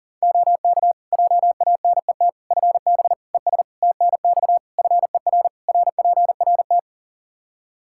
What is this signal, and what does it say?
Morse code, heard as a keyed tone, about 40 wpm, 700 Hz. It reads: OK JANET FB ES TNX FER RPRT
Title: Practice QSO - Exchange 2 - Info - 40wpm